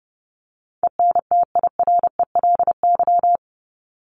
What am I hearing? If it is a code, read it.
Morse code, heard as a keyed tone, about 30 wpm, 700 Hz. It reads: ENTIRELY
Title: LY Suffix - Top 100 Words with Base Words 30wpm